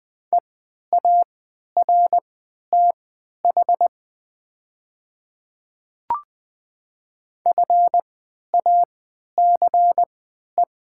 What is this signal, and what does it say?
Morse code, heard as a keyed tone, about 20 wpm, spaced as for 11 wpm, 690 Hz. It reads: EARTH FACE
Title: 3x Character Spacing - Top 300 Words - 20wpm